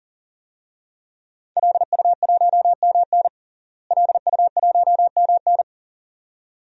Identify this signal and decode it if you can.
Morse code, heard as a keyed tone, about 40 wpm, 700 Hz. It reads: LU1MN LU1MN
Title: DXCC-100 Call Signs - 5 Characters - Repeated 2x - 40wpm